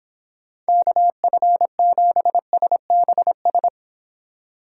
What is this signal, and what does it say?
Morse code, heard as a keyed tone, about 26 wpm, 700 Hz. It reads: KF7SBS